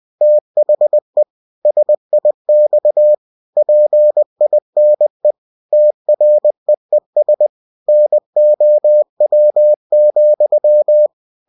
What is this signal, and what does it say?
Morse code, heard as a keyed tone, about 20 wpm, 600 Hz. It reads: THE SIX PINE TREES NOW,
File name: cw-books / winnie-the-pooh / winnie-20wpm-page-0041